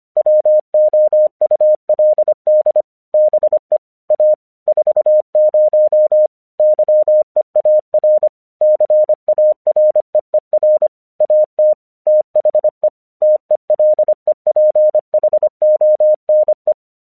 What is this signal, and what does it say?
Morse code, heard as a keyed tone, about 25 wpm, 610 Hz. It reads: WOULD BE A 40 YEAR CAREER AT THE TELEPHONE
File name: 1 English 25wpm / AQC10